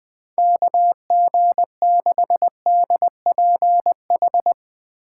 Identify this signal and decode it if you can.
Morse code, heard as a keyed tone, about 20 wpm, 700 Hz. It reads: KG6DPH
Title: USA Call Signs - 3x Character Spacing - 20wpm